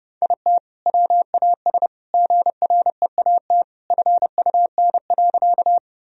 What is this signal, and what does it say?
Morse code, heard as a keyed tone, about 32 wpm, 700 Hz. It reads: IT WAS GREAT FUN.
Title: Sentences from Top 1000 Words - Review Part 4 - 30wpm